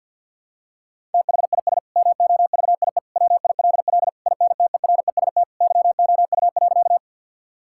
Morse code, heard as a keyed tone, about 50 wpm, 700 Hz. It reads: THIS MOVIE WILL INTEREST YOU.
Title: Sentences from Top 1000 Words - Review Part 5 - 50wpm